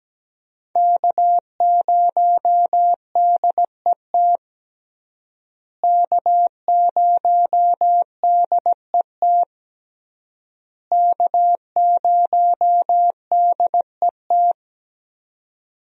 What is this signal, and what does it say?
Morse code, heard as a keyed tone, about 17 wpm, 700 Hz. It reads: K0DET K0DET K0DET
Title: USA Call Signs - 5 Characters - Repeated 3x - 17wpm